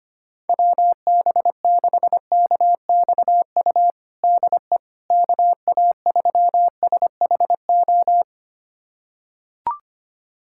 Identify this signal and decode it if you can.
Morse code, heard as a keyed tone, about 25 wpm, 700 Hz. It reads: WB6KXU DE KA3SHO
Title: USA Callsign DE Callsign 25wpm